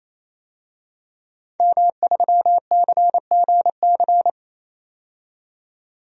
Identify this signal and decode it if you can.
Morse code, heard as a keyed tone, about 27 wpm, 700 Hz. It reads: M3CGC
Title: DXCC-100 Call Signs - 5 Characters - 28wpm